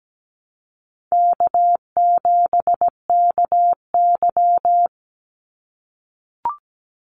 Morse code, heard as a keyed tone, about 17 wpm, 700 Hz. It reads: K7KY
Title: Easy Call Signs - 17wpm